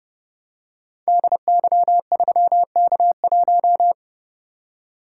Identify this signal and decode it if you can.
Morse code, heard as a keyed tone, about 30 wpm, 700 Hz. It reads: DY3K1
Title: Code Groups - 5 Letter:Numbers - 30wpm